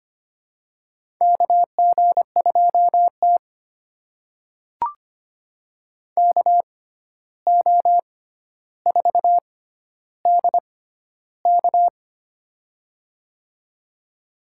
Morse code, heard as a keyed tone, about 25 wpm, spaced as for 12 wpm, 700 Hz. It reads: KG2T KO4DK
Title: USA Call Signs - 6x Character Spacing - 25wpm